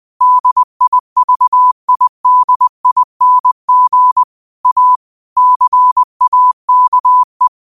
Morse code, heard as a keyed tone, about 20 wpm, 1000 Hz. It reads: DIVIDING A CAKE